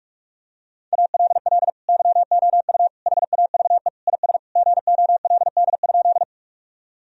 Morse code, heard as a keyed tone, about 47 wpm, 700 Hz. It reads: ALL YOU HAVE IS GOLD?
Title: Sentences from Top 1000 Words - Review Part 3 - 45wpm